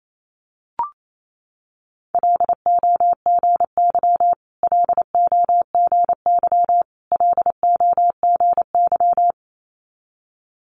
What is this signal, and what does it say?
Morse code, heard as a keyed tone, about 28 wpm, 700 Hz. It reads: LOGY LOGY LOGY
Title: Multi-letter Phonemes Repeated 3x 28wpm